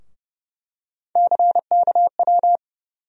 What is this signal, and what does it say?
Morse code, heard as a keyed tone, about 31 wpm, 700 Hz. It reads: CKW
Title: USA Call Sign Postfixes - 3 Letters 30wpm